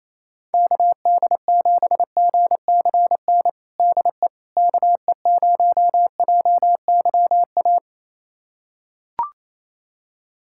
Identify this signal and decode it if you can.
Morse code, heard as a keyed tone, about 28 wpm, 700 Hz. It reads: KD7GCN DE KE0JYA
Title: USA Callsign DE Callsign 28wpm